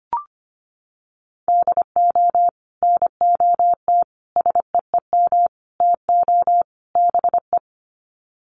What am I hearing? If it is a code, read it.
Morse code, heard as a keyed tone, about 25 wpm, 700 Hz. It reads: DO NOT SEEM TO BE
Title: Sets of 5 Words Encore2 25wpm